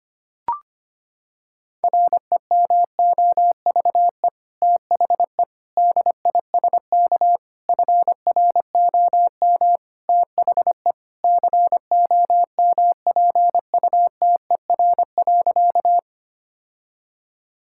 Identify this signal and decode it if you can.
Morse code, heard as a keyed tone, about 25 wpm, 700 Hz. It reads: REMOVE THE DISK FROM THE COMPUTER.
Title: Sentences from Top 1000 Words - Review Part 6 - 25wpm